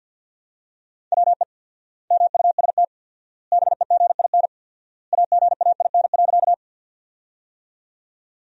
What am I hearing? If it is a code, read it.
Morse code, heard as a keyed tone, about 53 wpm, 700 Hz. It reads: WE MUST BEGIN AGAIN.
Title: 4x Word-Spacing - Sentences from Top 300 Words - 50wpm